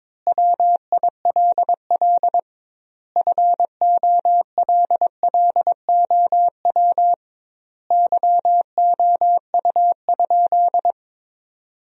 Morse code, heard as a keyed tone, about 22 wpm, 700 Hz. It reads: WILL FOLLOW YOU?
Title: 2x Word-Spacing - Sentences from Top 200 Words - 22wpm